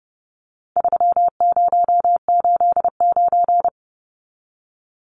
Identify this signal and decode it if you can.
Morse code, heard as a keyed tone, about 30 wpm, 700 Hz. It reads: 3089
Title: Numbers - 4-Digit 30wpm